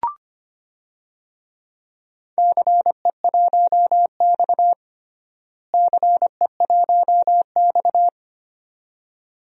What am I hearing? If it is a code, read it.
Morse code, heard as a keyed tone, about 25 wpm, 700 Hz. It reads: CE1X CE1X
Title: DXCC-100 Call Signs - 4 Characters - Repeated 2x - 25wpm